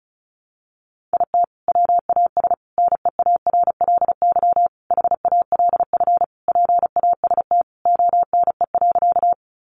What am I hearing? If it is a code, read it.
Morse code, heard as a keyed tone, about 34 wpm, 700 Hz. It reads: IT WAS NEARLY HALF PAST ONE.
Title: Sentences from Top 1300 Words 35wpm